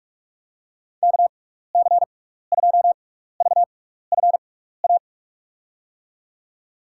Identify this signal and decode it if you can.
Morse code, heard as a keyed tone, about 46 wpm, spaced as for 15 wpm, 700 Hz. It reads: KC2VFA
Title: USA Call Signs - 6x Character Spacing - 45wpm